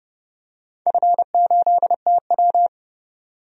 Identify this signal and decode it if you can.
Morse code, heard as a keyed tone, about 30 wpm, 700 Hz. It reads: F8TW
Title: DXCC-100 Call Signs - 4 Characters - 30wpm